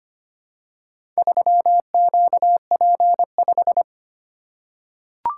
Morse code, heard as a keyed tone, about 25 wpm, 700 Hz. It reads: 3QP5
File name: practice / 25wpm / Four Letter-Number Combinations - 25wpm